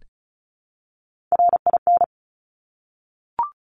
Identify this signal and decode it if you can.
Morse code, heard as a keyed tone, about 36 wpm, 700 Hz. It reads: RIN
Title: Top Three Letter Combinations in Words 35wpm